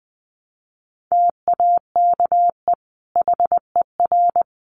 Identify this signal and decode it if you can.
Morse code, heard as a keyed tone, about 20 wpm, 700 Hz. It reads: TAKE HER
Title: Sentences from Top 1100 Words 20wpm